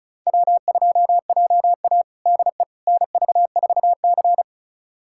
Morse code, heard as a keyed tone, about 34 wpm, 700 Hz. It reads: W2JA DE NV4C
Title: USA Callsign DE Callsign - 4 Characters - 35wpm